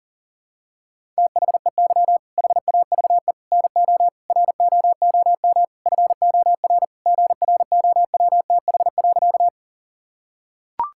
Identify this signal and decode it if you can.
Morse code, heard as a keyed tone, about 40 wpm, 690 Hz. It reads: THEY HAVE NO ROOM FOR GROWTH.
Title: Sentences from Top 1000 Words - Review Part 5 - 40wpm